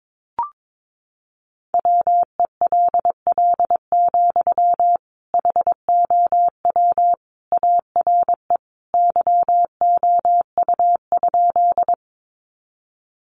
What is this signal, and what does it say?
Morse code, heard as a keyed tone, about 22 wpm, 700 Hz. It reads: WELL, HOW ARE YOU?
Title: Sentences from Top 200 Words 22wpm